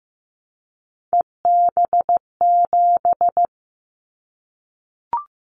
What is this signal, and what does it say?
Morse code, heard as a keyed tone, about 15 wpm, 700 Hz. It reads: EB7
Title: Callsign Prefixes for Top 25 DXCC - 15wpm